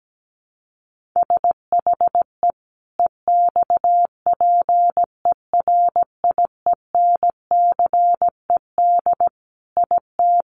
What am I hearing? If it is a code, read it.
Morse code, heard as a keyed tone, about 17 wpm, 700 Hz. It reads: SHE EXPERIENCED IT
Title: Sentences from Top 1100 Words 17wpm